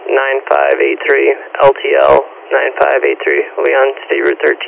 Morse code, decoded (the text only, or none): none